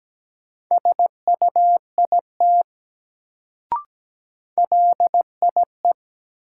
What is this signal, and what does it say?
Morse code, heard as a keyed tone, about 17 wpm, 700 Hz. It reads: SUIT LIE
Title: Sentences from Top 1600 Words - New Word Drill - 17wpm